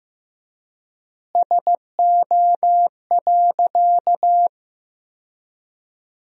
Morse code, heard as a keyed tone, about 16 wpm, 700 Hz. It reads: SO.
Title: 3x Word-Spacing - Sentences from Top 300 Words - 15wpm